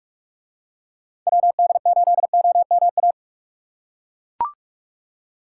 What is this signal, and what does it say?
Morse code, heard as a keyed tone, about 41 wpm, 700 Hz. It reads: WD8OMA